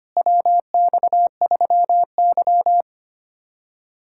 Morse code, heard as a keyed tone, about 25 wpm, 700 Hz. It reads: WX3Y
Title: USA Callsign DE Callsign - 4 Characters - 25wpm